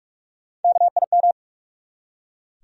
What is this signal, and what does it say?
Morse code, heard as a keyed tone, about 43 wpm, 700 Hz. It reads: KIM
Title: USA Common Names - Max 3 Letters - 45wpm